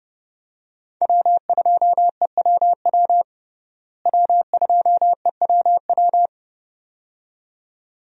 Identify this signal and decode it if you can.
Morse code, heard as a keyed tone, about 30 wpm, 700 Hz. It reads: W2EWW W2EWW
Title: USA Call Signs - 5 Characters - Repeated 2x - 30wpm